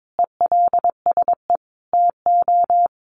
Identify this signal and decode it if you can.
Morse code, heard as a keyed tone, about 22 wpm, 700 Hz. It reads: ELSE TO
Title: Sentences from Top 1400 Words 22wpm